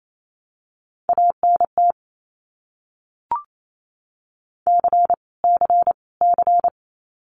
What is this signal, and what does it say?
Morse code, heard as a keyed tone, about 28 wpm, 700 Hz. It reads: ANT C C C
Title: Abbreviations Repeated 3x - 28wpm